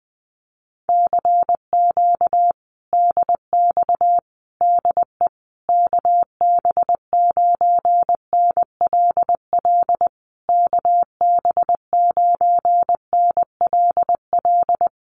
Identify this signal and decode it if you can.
Morse code, heard as a keyed tone, about 20 wpm, 700 Hz. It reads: CQ DX DE KB9NLL KB9NLL
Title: Common QSO Phrases Encore 20wpm